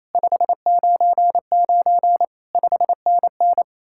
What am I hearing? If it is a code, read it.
Morse code, heard as a keyed tone, about 34 wpm, 700 Hz. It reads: 599 5NN
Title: Speed-Racing Common QSO Phrases Encore - 28wpm